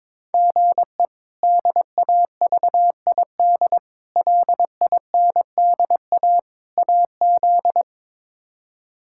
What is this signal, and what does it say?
Morse code, heard as a keyed tone, about 22 wpm, 700 Hz. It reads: GE DAVID LINDA AZ
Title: K1USN SST QSO Practice - 22wpm